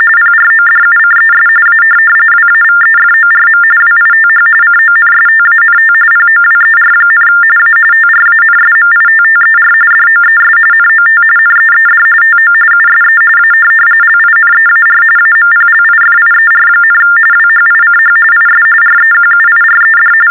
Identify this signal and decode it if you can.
TTY/TDD text telephone tones, about 45.45 baud. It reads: PRODUCES A WAVEFORM WITH CONSTANT POWER, A TRANSMITTER DOES NOT NEED TO USE A LINEAR AMPLIFIER, WHICH IS REQUIRED
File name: teletype-RTTY-45